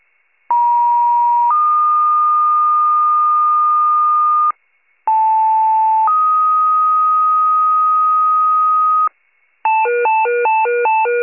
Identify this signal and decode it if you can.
Two-tone page, heard as a one-second tone, then a three-second tone, about 950 then 1220 Hz, about 880 then 1250 Hz. It is heard from 0.5 to 9.1 s.